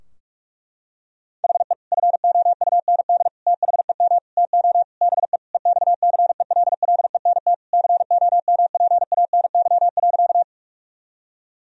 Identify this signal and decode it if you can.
Morse code, heard as a keyed tone, about 46 wpm, 700 Hz. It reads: HE FOUND THEM TO BE EXCELLENT COMPANY.
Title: Sentences from Top 1000 Words - Review Part 3 - 45wpm